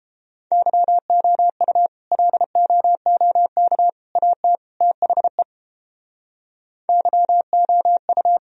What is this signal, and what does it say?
Morse code, heard as a keyed tone, about 30 wpm, 700 Hz. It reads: YOU LOOK AT THE YOU